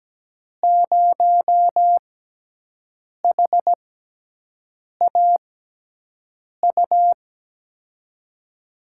Morse code, heard as a keyed tone, about 17 wpm, spaced as for 6 wpm, 700 Hz. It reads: 0HAU
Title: USA Call Signs - 5 Characters - 6x Character Spacing - 17wpm